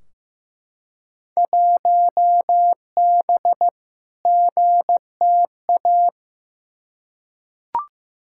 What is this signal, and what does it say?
Morse code, heard as a keyed tone, about 15 wpm, 700 Hz. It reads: 1B GTA
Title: Field Day Exchanges Encore - 15wpm